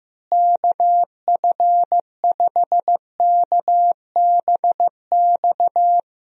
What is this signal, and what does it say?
Morse code, heard as a keyed tone, about 15 wpm, 700 Hz. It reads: KF5KBX
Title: USA Call Signs - 6 Characters - Repeated 2x - 15wpm